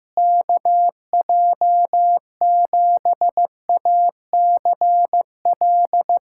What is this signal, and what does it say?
Morse code, heard as a keyed tone, about 15 wpm, 700 Hz. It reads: KJ7ACL
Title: USA Callsign DE Callsign 15wpm